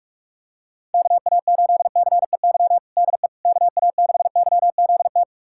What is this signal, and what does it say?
Morse code, heard as a keyed tone, about 46 wpm, 700 Hz. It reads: KA8CEY DE KA6YZT
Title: USA Callsign DE Callsign - 6 Characters - 45wpm